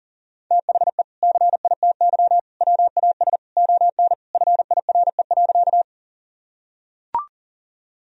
Morse code, heard as a keyed tone, about 40 wpm, 700 Hz. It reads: THE CITY WAS ON FIRE.
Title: Sentences from Top 1000 Words - Review Part 1 - 40wpm